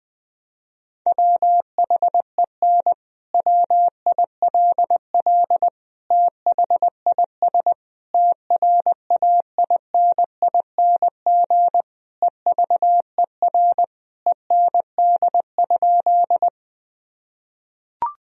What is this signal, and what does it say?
Morse code, heard as a keyed tone, about 20 wpm, 700 Hz. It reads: WHEN WILL THIS TRAINING EVER END?
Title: Sentences from Top 1000 Words - Review Part 4 - 20wpm